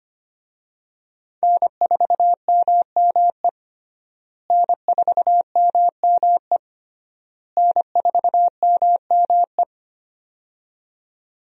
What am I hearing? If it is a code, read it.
Morse code, heard as a keyed tone, about 25 wpm, 700 Hz. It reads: N4MME N4MME N4MME
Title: USA Call Signs - 5 Characters - Repeated 3x - 25wpm